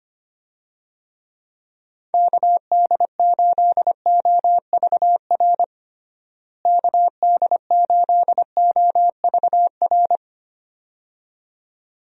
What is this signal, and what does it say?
Morse code, heard as a keyed tone, about 25 wpm, 700 Hz. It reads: KD8OVR KD8OVR
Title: USA Call Signs - 6 Characters - Repeated 2x - 25wpm